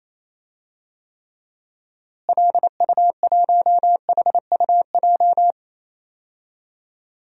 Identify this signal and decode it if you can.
Morse code, heard as a keyed tone, about 28 wpm, 700 Hz. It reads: LU1HUJ
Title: DXCC-100 Call Signs - 6 Characters - Repeated 3x - 28wpm